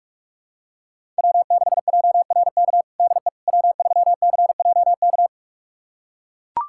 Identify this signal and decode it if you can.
Morse code, heard as a keyed tone, about 41 wpm, 700 Hz. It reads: W6JRK DE W3CJK
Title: USA Callsign DE Callsign - 5 Characters - 45wpm